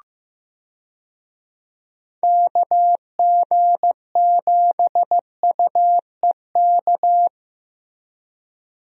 Morse code, heard as a keyed tone, about 15 wpm, 700 Hz. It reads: KG7UEK
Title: USA Call Signs - 6 Characters - Repeated 3x - 15wpm